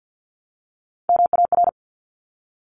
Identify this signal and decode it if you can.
Morse code, heard as a keyed tone, about 48 wpm, 700 Hz. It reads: MAR
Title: ARRL Sections 50wpm